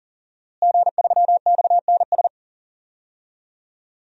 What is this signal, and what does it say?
Morse code, heard as a keyed tone, about 40 wpm, 700 Hz. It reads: G3XNS
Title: DXCC-100 Call Signs - 5 Characters - Repeated 3x - 40wpm